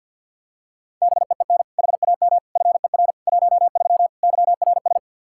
Morse code, heard as a keyed tone, about 49 wpm, 700 Hz. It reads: BEEN HAM FER 13 YRS
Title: Practice QSO - Exchange 3 - More Info - 50wpm